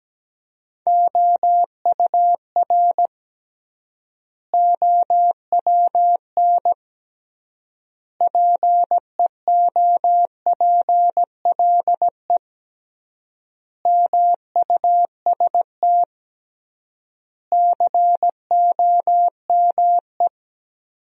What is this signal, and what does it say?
Morse code, heard as a keyed tone, about 17 wpm, 700 Hz. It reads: OUR OWN PEOPLE MUST COME
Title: 3x Word-Spacing - Sentences from Top 300 Words - 17wpm